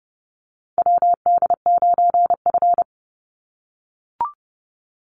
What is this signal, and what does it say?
Morse code, heard as a keyed tone, about 30 wpm, 700 Hz. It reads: WD9F
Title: NA SOTA Chaser Callsigns 30wpm